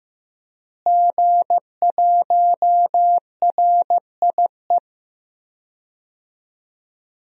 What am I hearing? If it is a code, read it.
Morse code, heard as a keyed tone, about 15 wpm, 700 Hz. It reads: G1RIE
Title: DXCC-100 Call Signs - 5 Characters - Repeated 3x - 15wpm